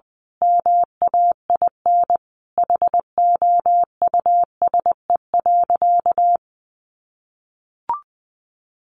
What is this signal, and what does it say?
Morse code, heard as a keyed tone, about 20 wpm, 700 Hz. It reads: MAIN HOUSE.
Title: Sentences from Top 1300 Words 20wpm